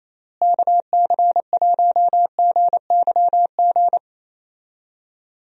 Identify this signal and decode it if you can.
Morse code, heard as a keyed tone, about 27 wpm, 700 Hz. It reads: KC1GYG
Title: USA Call Signs - 6 Characters - Repeated 2x - 28wpm